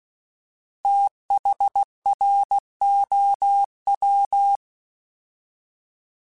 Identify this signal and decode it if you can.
Morse code, heard as a keyed tone, about 16 wpm, 800 Hz. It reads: THROW
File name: TH / TH mp3s / TH Prefixes spt @15